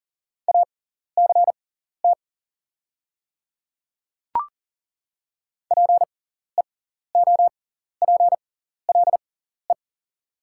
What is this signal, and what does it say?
Morse code, heard as a keyed tone, about 38 wpm, spaced as for 14 wpm, 700 Hz. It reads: ACT PEOPLE
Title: 6x Character Spacing - Top 200 Words - 40wpm